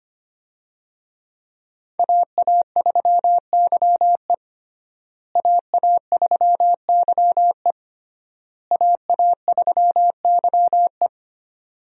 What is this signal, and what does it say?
Morse code, heard as a keyed tone, about 25 wpm, 700 Hz. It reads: AA3YE AA3YE AA3YE
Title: USA Call Signs - 5 Characters - Repeated 3x - 25wpm